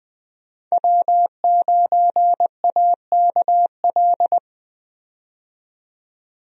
Morse code, heard as a keyed tone, about 20 wpm, 700 Hz. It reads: W9AKL
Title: USA Call Signs - 5 Characters - 20wpm